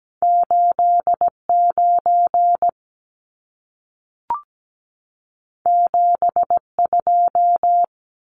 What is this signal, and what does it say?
Morse code, heard as a keyed tone, about 17 wpm, 700 Hz. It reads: 89 72
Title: Numbers - 2-Digit 17wpm